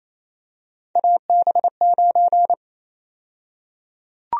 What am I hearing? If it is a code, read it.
Morse code, heard as a keyed tone, about 28 wpm, 700 Hz. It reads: AB9
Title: USA Call Sign Prefixes - 28wpm